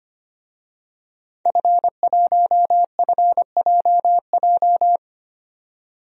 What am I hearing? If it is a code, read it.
Morse code, heard as a keyed tone, about 25 wpm, 700 Hz. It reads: F1FJJ
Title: DXCC-100 Call Signs - 5 Characters - 25wpm